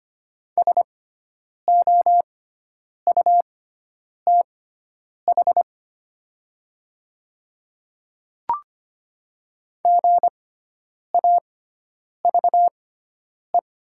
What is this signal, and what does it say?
Morse code, heard as a keyed tone, about 25 wpm, spaced as for 9 wpm, 700 Hz. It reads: SOUTH GAVE
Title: 6x Character Spacing - Top 500 Words - 25wpm